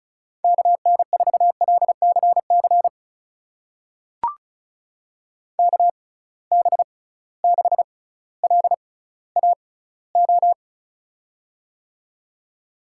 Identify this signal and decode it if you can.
Morse code, heard as a keyed tone, about 36 wpm, spaced as for 18 wpm, 700 Hz. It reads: KN4LCC KB6LAO
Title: USA Call Signs - 6 Characters - 6x Character Spacing - 35wpm